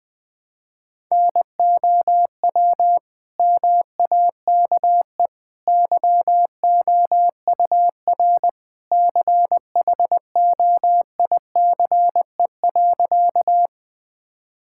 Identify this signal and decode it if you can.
Morse code, heard as a keyed tone, about 21 wpm, 700 Hz. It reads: NOW MAKE YOUR CHOICE.
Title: Sentences from Top 1000 Words - Review Part 6 - 20wpm